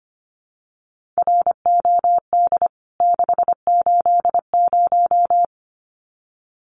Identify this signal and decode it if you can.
Morse code, heard as a keyed tone, about 25 wpm, 700 Hz. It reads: ROD 680
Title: CWT Exchange 25wpm